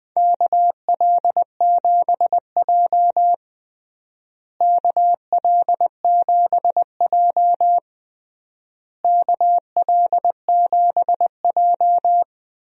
Morse code, heard as a keyed tone, about 20 wpm, 700 Hz. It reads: KL7J KL7J KL7J
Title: DXCC-100 Call Signs - 4 Characters - Repeated 3x - 20wpm